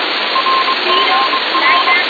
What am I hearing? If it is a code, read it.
Morse code, heard as a keyed tone, about 23 wpm, 1060 Hz, with noise. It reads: RGR